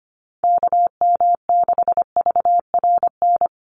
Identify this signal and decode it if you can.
Morse code, heard as a keyed tone, about 25 wpm, 700 Hz. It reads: KM6VRN
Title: USA Call Signs 25wpm